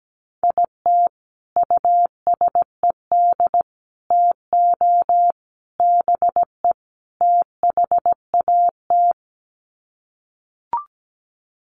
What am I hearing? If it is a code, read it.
Morse code, heard as a keyed tone, about 17 wpm, 700 Hz. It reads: IT USED TO BE THAT
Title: Sets of 5 Words Encore2 17wpm